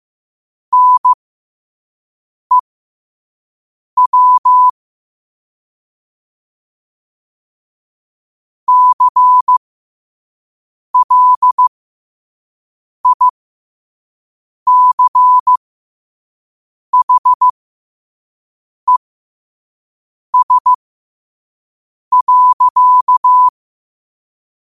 Morse code, heard as a keyed tone, about 15 wpm, spaced as for 5 wpm, 1000 Hz. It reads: NEW CLICHES.
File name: qotdfiles / 5wpm